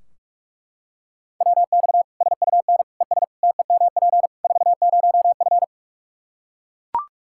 Morse code, heard as a keyed tone, about 41 wpm, 700 Hz. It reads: WX SUN ES TEMP 40F
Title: Practice QSO - Exchange 2 - Info - 45wpm